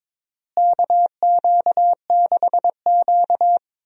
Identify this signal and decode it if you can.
Morse code, heard as a keyed tone, about 22 wpm, 700 Hz. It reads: KQ6Q
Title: USA Call Signs - 4 Characters - Repeated 3x - 22wpm